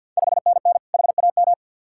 Morse code, heard as a keyed tone, about 52 wpm, 690 Hz. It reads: 5NN HAM
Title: Ontario QSO Party Exchange 50wpm